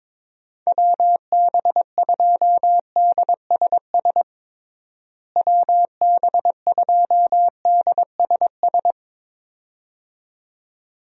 Morse code, heard as a keyed tone, about 21 wpm, 700 Hz. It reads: WB2DSS WB2DSS
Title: USA Call Signs - 6 Characters - Repeated 2x - 22wpm